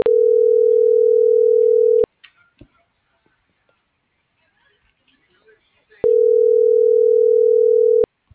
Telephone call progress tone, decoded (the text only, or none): ringback tone